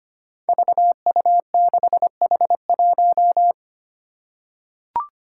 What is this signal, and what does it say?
Morse code, heard as a keyed tone, about 25 wpm, 700 Hz. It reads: VU6H1
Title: Five Challenging Letter-Number Combinations 25wpm